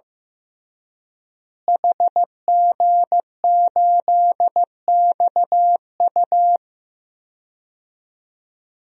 Morse code, heard as a keyed tone, about 16 wpm, 700 Hz. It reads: HG8XU
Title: DXCC-100 Call Signs - 5 Characters - Repeated 3x - 15wpm